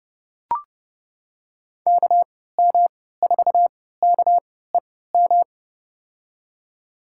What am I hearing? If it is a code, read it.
Morse code, heard as a keyed tone, about 30 wpm, spaced as for 17 wpm, 700 Hz. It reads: KM4KEM
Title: USA Call Signs - 3x Character Spacing - 30wpm